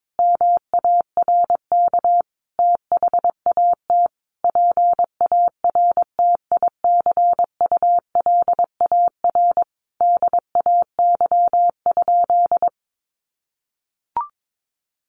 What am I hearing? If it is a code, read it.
Morse code, heard as a keyed tone, about 22 wpm, 700 Hz. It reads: MARK THAT PARTICULAR DAY?
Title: Sentences from Top 1400 Words 22wpm